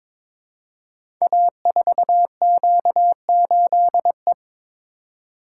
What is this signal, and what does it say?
Morse code, heard as a keyed tone, about 22 wpm, 700 Hz. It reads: A4Q8E
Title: Code Groups - 5 Letter:Numbers - 22wpm